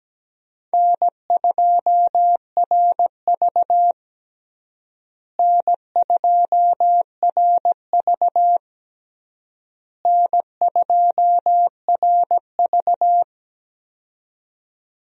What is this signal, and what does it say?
Morse code, heard as a keyed tone, about 17 wpm, 700 Hz. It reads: N2RV N2RV N2RV
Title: USA Call Signs - 4 Characters - Repeated 3x - 17wpm